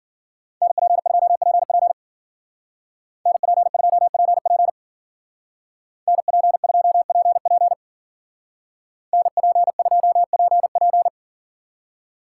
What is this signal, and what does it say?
Morse code, heard as a keyed tone, about 48 wpm, 700 Hz. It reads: NP2PP NP2PP NP2PP NP2PP
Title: Speed Racing - USA 2x2 Call Signs 40wpm